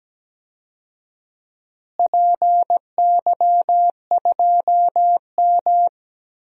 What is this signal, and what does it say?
Morse code, heard as a keyed tone, about 17 wpm, 700 Hz. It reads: PY2M
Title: DXCC-100 Call Signs - 4 Characters - Repeated 2x - 17wpm